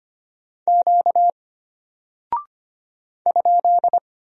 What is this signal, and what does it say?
Morse code, heard as a keyed tone, about 25 wpm, 700 Hz. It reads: Q ?
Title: Single Character (Letter:Number:Punctuation:Prosign) Alt 25wpm